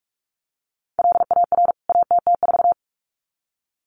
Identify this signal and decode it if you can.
Morse code, heard as a keyed tone, about 44 wpm, 700 Hz. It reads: LAR ATT4
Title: CWT Exchange 45wpm